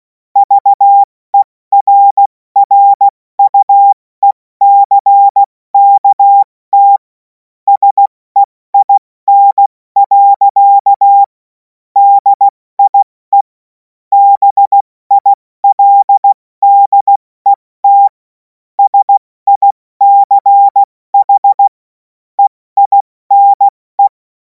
Morse code, heard as a keyed tone, about 16 wpm, 800 Hz. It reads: VERRUECKT SEIN. DIE BILDET SICH EINE
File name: Morsekurs / Witze / Witz 26. Mai 2026 70 F80